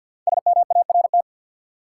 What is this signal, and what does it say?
Morse code, heard as a keyed tone, about 47 wpm, 700 Hz. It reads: SMART
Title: Five Letter Words - 50wpm